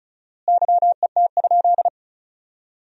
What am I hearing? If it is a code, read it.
Morse code, heard as a keyed tone, about 34 wpm, 700 Hz. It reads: YET?
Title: 4x Word-Spacing - Sentences from Top 500 Words - 35wpm